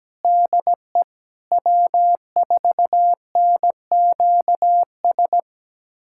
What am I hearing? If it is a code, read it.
Morse code, heard as a keyed tone, about 17 wpm, 700 Hz. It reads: DE W4NQS
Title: USA Callsign DE Callsign 17wpm